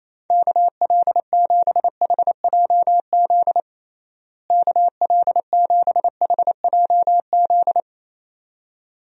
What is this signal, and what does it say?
Morse code, heard as a keyed tone, about 28 wpm, 700 Hz. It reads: KL7HJZ KL7HJZ
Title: DXCC-100 Call Signs - 6 Characters - Repeated 2x - 28wpm